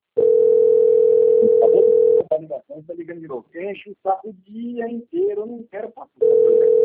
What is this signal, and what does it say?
Ringback tone (North America), 440 + 480 Hz, with speech.
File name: out- - - - - .19636